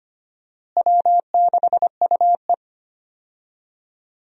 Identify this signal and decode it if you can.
Morse code, heard as a keyed tone, about 25 wpm, 700 Hz. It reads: W6UE
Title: USA Call Signs - 4 Characters - Repeated 3x - 25wpm